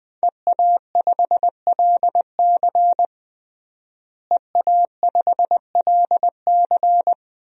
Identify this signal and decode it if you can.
Morse code, heard as a keyed tone, about 20 wpm, 700 Hz. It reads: EA5LC EA5LC
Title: DXCC-100 Call Signs - 5 Characters - Repeated 2x - 20wpm